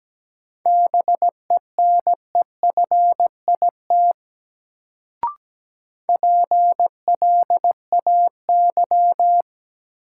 Morse code, heard as a keyed tone, about 17 wpm, 700 Hz. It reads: BENEFIT PLAY
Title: Sets of 5 Words Encore - New Word Drill - 17wpm